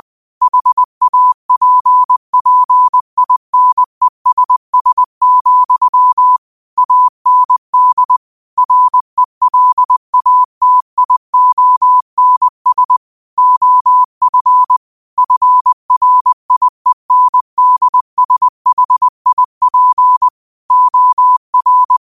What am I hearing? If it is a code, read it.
Morse code, heard as a keyed tone, about 20 wpm, 1000 Hz. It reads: HAPPINESS, AND RELATIONS OF FRIENDSHIP OR
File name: qotdfiles / 20wpm